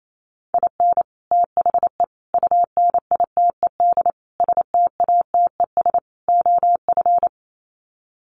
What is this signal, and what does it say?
Morse code, heard as a keyed tone, about 28 wpm, 700 Hz. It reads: IN THE UNITED STATES OF
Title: Sets of 5 Words Encore2 28wpm